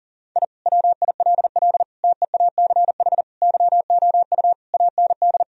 Morse code, heard as a keyed tone, about 40 wpm, 700 Hz. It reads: I WILL TEACH YOU AND
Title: Sentences from Top 1000 Words - Review Part 3 - 40wpm